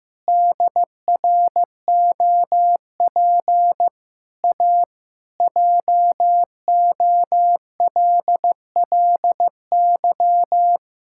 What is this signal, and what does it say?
Morse code, heard as a keyed tone, about 16 wpm, 700 Hz. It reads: DROP A JOLLY
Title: Pangram Challenge v2 - 15wpm